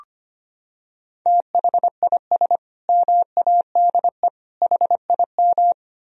Morse code, heard as a keyed tone, about 25 wpm, 700 Hz. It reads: THIS MADE HIM
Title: Sentences from Top 1000 Words - Review Part 3 - 25wpm